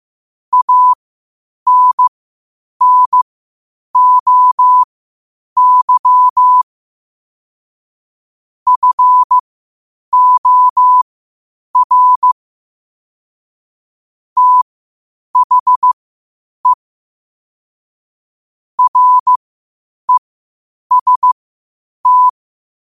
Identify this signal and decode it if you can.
Morse code, heard as a keyed tone, about 15 wpm, spaced as for 8 wpm, 1000 Hz. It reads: ANNOY FOR THE REST
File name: qotdfiles / 8wpm